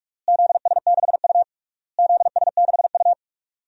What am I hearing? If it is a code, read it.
Morse code, heard as a keyed tone, about 41 wpm, 700 Hz. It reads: ZS6U ZS6U
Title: DXCC-100 Call Signs - 4 Characters - Repeated 3x - 45wpm